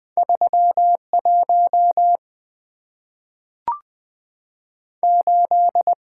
Morse code, heard as a keyed tone, about 20 wpm, 700 Hz. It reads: 31 8
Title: Two Letter-Number Combinations - 20wpm